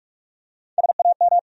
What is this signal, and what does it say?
Morse code, heard as a keyed tone, about 46 wpm, 700 Hz. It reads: SAM
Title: USA Common Names - Max 5 Letters - 45wpm